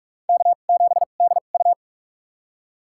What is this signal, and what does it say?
Morse code, heard as a keyed tone, about 46 wpm, spaced as for 32 wpm, 700 Hz. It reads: K7DU
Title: USA Call Signs - 4 Characters - 2x Character Spacing - 45wpm